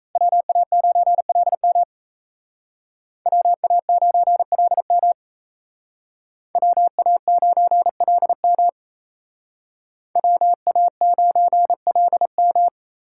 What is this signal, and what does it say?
Morse code, heard as a keyed tone, about 34 wpm, 700 Hz. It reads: WA9LM WA9LM WA9LM WA9LM
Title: Speed Racing - USA 2x2 Call Signs 28wpm